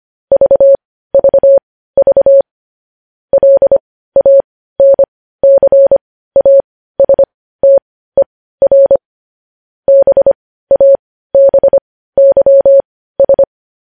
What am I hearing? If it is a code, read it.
Morse code, heard as a keyed tone, about 25 wpm, spaced as for 15 wpm, 550 Hz. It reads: VVV LANCASTER BABYS
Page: News Headlines in Morse Code at 15 WPM